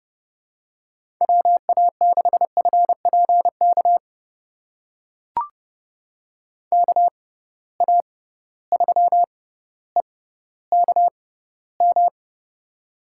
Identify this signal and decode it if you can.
Morse code, heard as a keyed tone, about 31 wpm, spaced as for 16 wpm, 700 Hz. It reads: WA6FPK KA3EKM
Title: USA Call Signs - 6 Characters - 6x Character Spacing - 30wpm